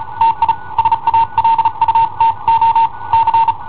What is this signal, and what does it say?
Morse code, heard as a keyed tone, about 38 wpm, spaced as for 32 wpm, 910 Hz. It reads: TI SALUTO C
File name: i7ohp / cassetta i7ohp B